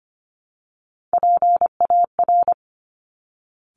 Morse code, heard as a keyed tone, about 25 wpm, 700 Hz. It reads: PAR
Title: Top Three Letter Combinations in Words 25wpm